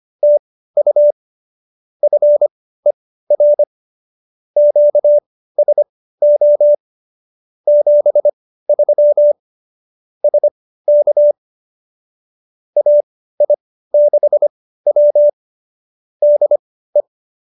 Morse code, heard as a keyed tone, about 25 wpm, spaced as for 15 wpm, 590 Hz. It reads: TU FER QSO 73 SK AI6W DE